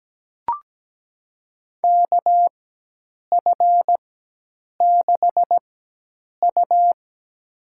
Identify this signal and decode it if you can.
Morse code, heard as a keyed tone, about 17 wpm, spaced as for 8 wpm, 700 Hz. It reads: KF6U
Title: USA Call Signs - 4 Characters - 4x Character Spacing - 17wpm